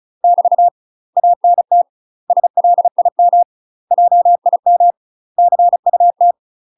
Morse code, heard as a keyed tone, about 34 wpm, 700 Hz. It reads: <BT> ANT SLIM JIM CUT
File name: UK MASTER QSOs TRAINING / 35WPM / UK MASTER QSOs 0000